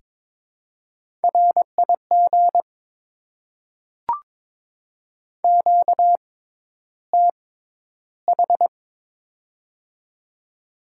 Morse code, heard as a keyed tone, about 22 wpm, spaced as for 11 wpm, 690 Hz. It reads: RIG QTH
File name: Top 20 QSO Elements - 6x Character Spacing - 22wpm